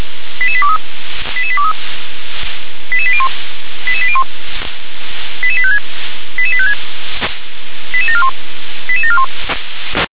Five-tone selective call (ZVEI), heard as quick runs of five tones, tones of about 70 ms. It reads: 88823 88823 88801 88801 88856 88856 88741 88741